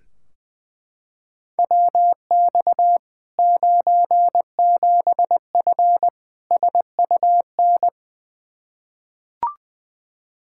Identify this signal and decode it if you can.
Morse code, heard as a keyed tone, about 21 wpm, 700 Hz. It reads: WX 97F SUN
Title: Practice QSO - Exchange 2 - Info - 20wpm